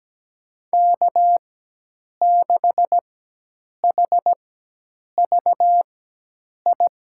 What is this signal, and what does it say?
Morse code, heard as a keyed tone, about 17 wpm, spaced as for 8 wpm, 700 Hz. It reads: K6HVI
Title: USA Call Signs - 5 Characters - 4x Character Spacing - 17wpm